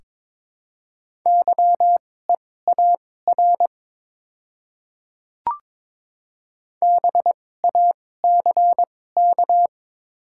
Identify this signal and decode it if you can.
Morse code, heard as a keyed tone, about 22 wpm, spaced as for 16 wpm, 700 Hz. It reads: YEAR BACK
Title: 2x Character Spacing - Top 300 Words - 22wpm